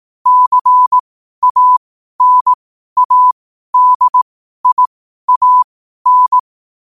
Morse code, heard as a keyed tone, about 18 wpm, spaced as for 12 wpm, 1000 Hz. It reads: CANADIAN
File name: qotdfiles / 12wpm